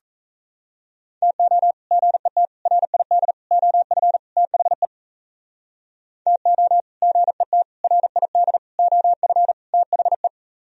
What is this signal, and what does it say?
Morse code, heard as a keyed tone, about 37 wpm, 700 Hz. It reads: TO GET RID OF THE TO GET RID OF THE